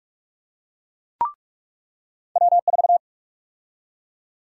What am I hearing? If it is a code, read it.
Morse code, heard as a keyed tone, about 42 wpm, 700 Hz. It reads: W4
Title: USA Call Sign Prefixes - 45wpm